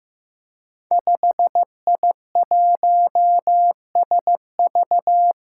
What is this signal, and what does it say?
Morse code, heard as a keyed tone, about 15 wpm, 700 Hz. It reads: 5I1SV
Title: Five Challenging Letter-Number Combinations 15wpm